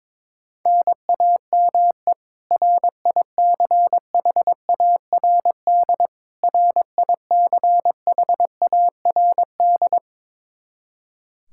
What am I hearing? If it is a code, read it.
Morse code, heard as a keyed tone, about 21 wpm, 700 Hz. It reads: NAME RICHARD RICHARD
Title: Practice QSO - Exchange 1 - Beginning - 22wpm